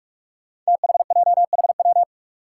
Morse code, heard as a keyed tone, about 41 wpm, 700 Hz. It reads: THJHW
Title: Five Challenging Letter-Number Combinations 45wpm